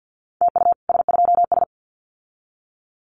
Morse code, heard as a keyed tone, about 50 wpm, 700 Hz. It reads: TU S2S
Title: Common QSO Phrases Encore 50wpm